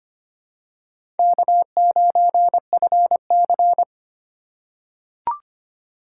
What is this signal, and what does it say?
Morse code, heard as a keyed tone, about 25 wpm, 700 Hz. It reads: K9FC
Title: USA Call Signs - 4 Characters - 3x Character Spacing - 25wpm